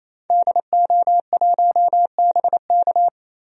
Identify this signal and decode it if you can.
Morse code, heard as a keyed tone, about 28 wpm, 690 Hz. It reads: DO1BK
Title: DXCC-100 Call Signs - 5 Characters - Repeated 3x - 28wpm